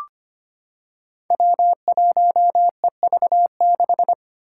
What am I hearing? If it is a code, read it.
Morse code, heard as a keyed tone, about 25 wpm, 700 Hz. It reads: W1EV6
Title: Five Challenging Letter-Number Combinations 25wpm